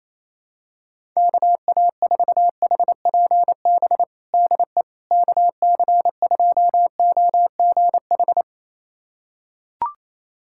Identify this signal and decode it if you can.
Morse code, heard as a keyed tone, about 28 wpm, 700 Hz. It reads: KA4HPB DE KC2OGH
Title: USA Callsign DE Callsign 28wpm